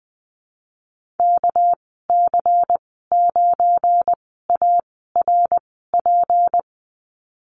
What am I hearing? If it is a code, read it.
Morse code, heard as a keyed tone, about 20 wpm, spaced as for 15 wpm, 700 Hz. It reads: KC9ARP
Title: 231 Lesson 37g - Letter x - Call Signs 20wpm